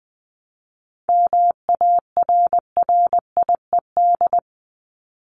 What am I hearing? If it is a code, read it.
Morse code, heard as a keyed tone, about 20 wpm, 700 Hz. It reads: MARRIED
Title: Top 1501-1750 Words 20wpm